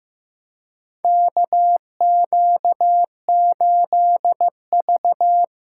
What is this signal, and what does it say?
Morse code, heard as a keyed tone, about 15 wpm, 700 Hz. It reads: KQ8V
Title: USA Callsign DE Callsign - 4 Characters - 15wpm